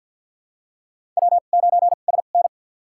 Morse code, heard as a keyed tone, about 51 wpm, spaced as for 36 wpm, 700 Hz. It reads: W9SN
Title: USA Call Signs - 4 Characters - 2x Character Spacing - 50wpm